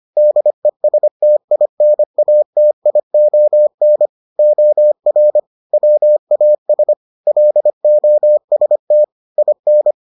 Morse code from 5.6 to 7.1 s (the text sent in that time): WAS